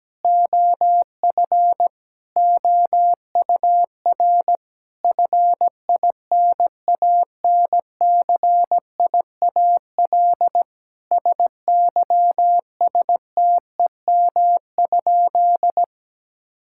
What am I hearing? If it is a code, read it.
Morse code, heard as a keyed tone, about 17 wpm, 700 Hz. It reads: OF OUR FINANCIAL SYSTEM?
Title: Sentences from Top 1000 Words - Review Part 5 - 17wpm